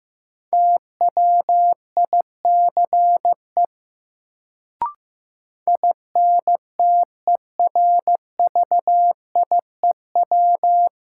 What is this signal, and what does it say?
Morse code, heard as a keyed tone, about 15 wpm, 700 Hz. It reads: TWICE INTERVIEW
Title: Sentences from Top 1600 Words - New Word Drill - 15wpm